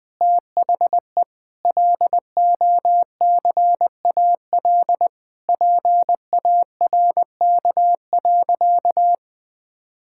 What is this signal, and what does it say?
Morse code, heard as a keyed tone, about 20 wpm, 700 Hz. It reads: THE LOCAL PARK.
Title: Difficult Sentences with Hints - 20wpm